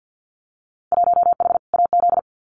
Morse code, heard as a keyed tone, about 50 wpm, 700 Hz. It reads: 1H AZ
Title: Winter Field Day Exchanges 50wpm